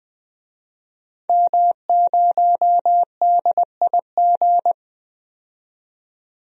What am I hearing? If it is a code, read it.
Morse code, heard as a keyed tone, about 21 wpm, 700 Hz. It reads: M0DIG
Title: DXCC-100 Call Signs - 5 Characters - 20wpm